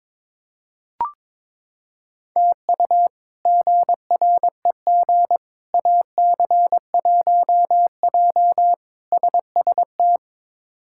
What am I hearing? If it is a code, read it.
Morse code, heard as a keyed tone, about 22 wpm, 700 Hz. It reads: TU GREG AC1J SST
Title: K1USN SST QSO Practice - 22wpm